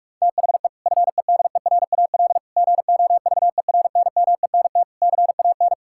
Morse code, heard as a keyed tone, about 46 wpm, 700 Hz. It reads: THE FEDERAL GOVERNMENT CAN
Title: Sentences from Top 1000 Words - Review Part 5 - 45wpm